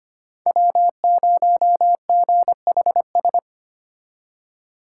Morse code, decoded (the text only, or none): W0GHS